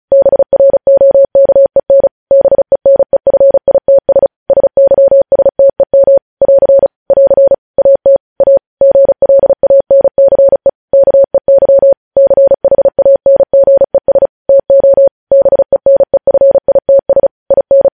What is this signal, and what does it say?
Morse code, heard as a keyed tone, about 36 wpm, 550 Hz. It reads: BROKEN BENEFITS SYSTEM <AR> <AR> AT A GLANCE KEY CHANGES TO BENEFITS IN
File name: podcast / News headlines-1x-35x35